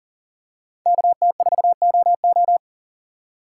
Morse code, heard as a keyed tone, about 37 wpm, 700 Hz. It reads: KT4OO
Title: USA Call Signs - 5 Characters - 2x Character Spacing - 40wpm